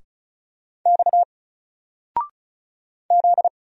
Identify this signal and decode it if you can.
Morse code, heard as a keyed tone, about 36 wpm, 700 Hz. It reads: X Z
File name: Single-Letter-Number-35wpm